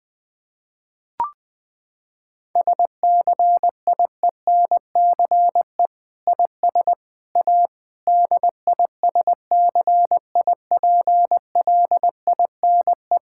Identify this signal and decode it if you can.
Morse code, heard as a keyed tone, about 20 wpm, 700 Hz. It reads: SCIENCE IS A DISCIPLINE
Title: Sentences from Top 1000 Words - Review Part 6 - 20wpm